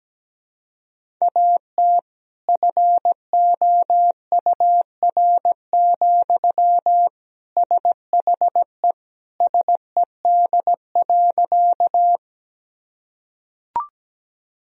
Morse code, heard as a keyed tone, about 17 wpm, 700 Hz. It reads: AT FOUR, SHE SED.
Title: Sentences from Top 300 Words and Abbreviations 17wpm